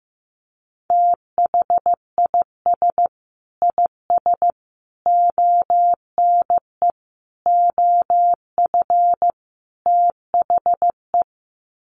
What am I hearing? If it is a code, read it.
Morse code, heard as a keyed tone, about 14 wpm, 700 Hz. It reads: THIS IS ONE OF THE
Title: Sets of 5 Words 15wpm